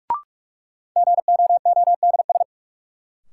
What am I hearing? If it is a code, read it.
Morse code, heard as a keyed tone, about 41 wpm, 700 Hz. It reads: GOODS
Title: Sentences from Top 1000 Words - Review Part 6 - New Word Drill - 45wpm